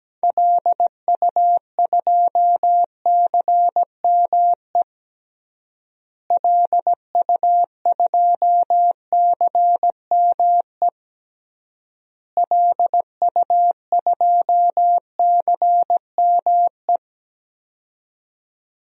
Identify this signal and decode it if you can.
Morse code, heard as a keyed tone, about 17 wpm, 700 Hz. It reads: LU2CME LU2CME LU2CME
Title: DXCC-100 Call Signs - 6 Characters - Repeated 3x - 17wpm